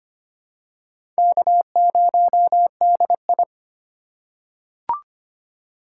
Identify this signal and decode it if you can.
Morse code, heard as a keyed tone, about 25 wpm, 700 Hz. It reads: K0DI
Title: USA Call Signs - 3x Character Spacing - 25wpm